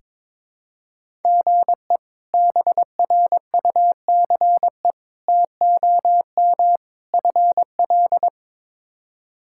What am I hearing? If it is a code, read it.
Morse code, heard as a keyed tone, about 22 wpm, 700 Hz. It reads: GE BRUCE TOM FL
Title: K1USN SST QSO Practice - 22wpm